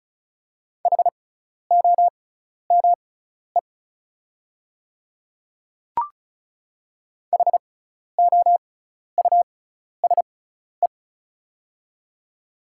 Morse code, heard as a keyed tone, about 36 wpm, spaced as for 12 wpm, 690 Hz. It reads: HOME HOUSE
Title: 6x Character Spacing - Top 500 Words - 35wpm